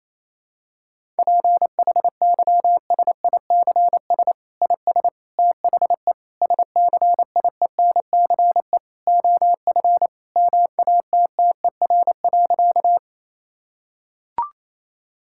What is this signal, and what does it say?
Morse code, heard as a keyed tone, about 28 wpm, 700 Hz. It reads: PHYSICS IS THE SCIENCE OF MATTER.
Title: Sentences from Top 1000 Words - Review Part 5 - 28wpm